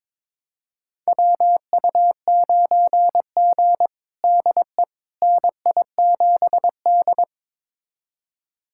Morse code, heard as a keyed tone, about 22 wpm, 700 Hz. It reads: WU9G DE NI7D
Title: USA Callsign DE Callsign - 4 Characters - 22wpm